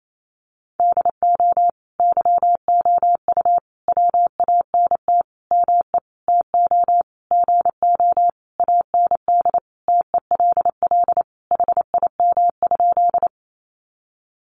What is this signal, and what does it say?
Morse code, heard as a keyed tone, about 28 wpm, 700 Hz. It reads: DO YOU WANT ME TO GO AND TELL HIM?
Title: Sentences from Top 200 Words 28wpm